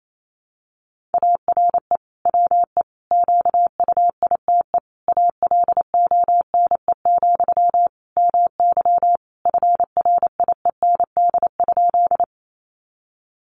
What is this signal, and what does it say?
Morse code, heard as a keyed tone, about 28 wpm, 700 Hz. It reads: ARE WE QUITE ALONE, MY FRIEND?
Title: Sentences from Top 1000 Words 28wpm